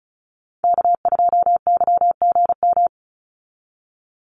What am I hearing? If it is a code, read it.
Morse code, heard as a keyed tone, about 34 wpm, 700 Hz. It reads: K2YGM
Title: Call Signs Repeated 2x - 35wpm